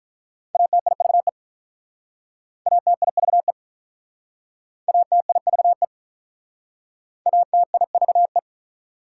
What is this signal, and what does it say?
Morse code, heard as a keyed tone, about 47 wpm, 700 Hz. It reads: ATIVE ATIVE ATIVE ATIVE